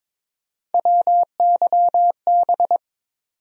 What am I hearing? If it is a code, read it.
Morse code, heard as a keyed tone, about 22 wpm, 700 Hz. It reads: WYB
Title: USA Call Sign Postfixes - 3 Letters 22wpm